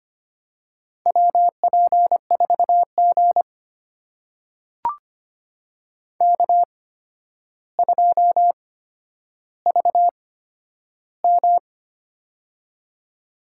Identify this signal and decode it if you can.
Morse code, heard as a keyed tone, about 25 wpm, spaced as for 11 wpm, 700 Hz. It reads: WP4G K2VM
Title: USA Call Signs - 4 Characters - 8x Character Spacing - 25wpm